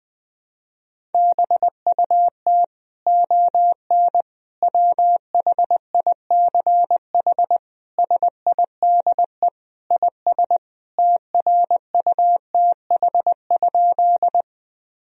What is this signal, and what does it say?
Morse code, heard as a keyed tone, about 21 wpm, 700 Hz. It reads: BUT ON WHICH SIDE IS TRUTH?
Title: Sentences from Top 1000 Words - Review Part 1 - 20wpm